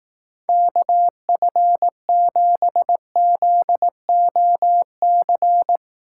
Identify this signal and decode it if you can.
Morse code, heard as a keyed tone, about 18 wpm, 700 Hz. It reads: KF7ZOC
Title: Speed Racing - USA 2x3 Call Signs 15wpm